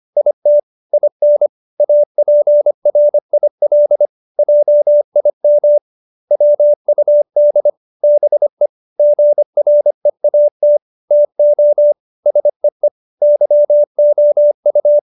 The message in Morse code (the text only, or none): IT IN APRIL JIM WUD BE GREAT TO SEE YOU